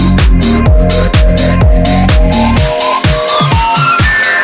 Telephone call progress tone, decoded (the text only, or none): reorder tone